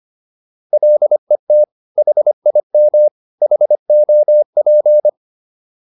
Morse code, heard as a keyed tone, about 25 wpm, 600 Hz. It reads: LET HIM HOP